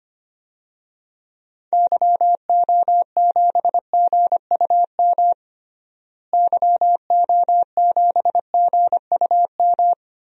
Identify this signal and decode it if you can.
Morse code, heard as a keyed tone, about 25 wpm, 700 Hz. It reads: YO7GUM YO7GUM
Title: DXCC-100 Call Signs - 6 Characters - Repeated 2x - 25wpm